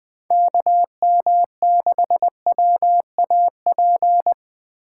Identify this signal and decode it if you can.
Morse code, heard as a keyed tone, about 20 wpm, 700 Hz. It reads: KM6WAP
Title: Speed Racing - USA Call Signs 17wpm